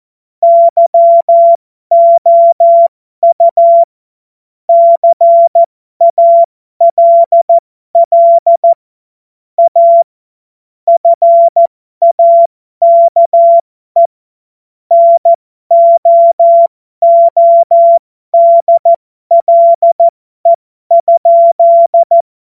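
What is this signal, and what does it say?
Morse code, heard as a keyed tone, about 14 wpm, spaced as for 12 wpm, 680 Hz. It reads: YOU CALL A FAKE NOODLE?
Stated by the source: Amateur radio CW practice text made of jokes.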